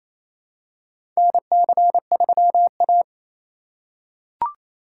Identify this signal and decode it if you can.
Morse code, heard as a keyed tone, about 28 wpm, 700 Hz. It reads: NC3A
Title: USA Call Signs - 4 Characters - 6x Character Spacing - 28wpm